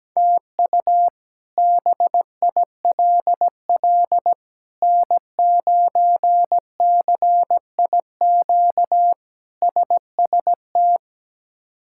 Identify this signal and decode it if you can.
Morse code, heard as a keyed tone, about 17 wpm, 700 Hz. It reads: TU BILL N9CIQ SST
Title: K1USN SST QSO Practice - 17wpm